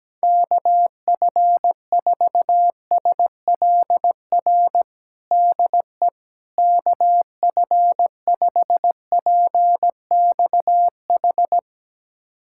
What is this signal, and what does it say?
Morse code, heard as a keyed tone, about 17 wpm, 700 Hz. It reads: KF4SLR DE KF5PXH
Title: USA Callsign DE Callsign - 6 Characters - 17wpm